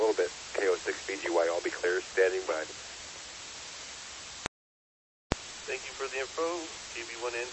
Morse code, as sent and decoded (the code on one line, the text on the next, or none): none